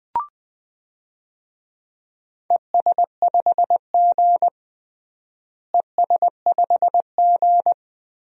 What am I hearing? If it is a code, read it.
Morse code, heard as a keyed tone, about 20 wpm, 700 Hz. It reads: ES5G ES5G
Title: DXCC-100 Call Signs - 4 Characters - Repeated 3x - 20wpm